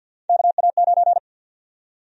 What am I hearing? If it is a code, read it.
Morse code, heard as a keyed tone, about 52 wpm, 700 Hz. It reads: KA9
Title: USA Call Sign Prefixes - 50wpm